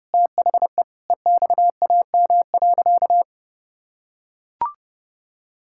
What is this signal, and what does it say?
Morse code, heard as a keyed tone, about 30 wpm, 700 Hz. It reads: THE EXAM.
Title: Sentences from Top 1000 Words - Review Part 5 - 30wpm